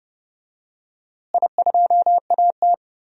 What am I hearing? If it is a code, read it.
Morse code, heard as a keyed tone, about 30 wpm, 700 Hz. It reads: I2AT
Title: DXCC-100 Call Signs - 4 Characters - Repeated 2x - 30wpm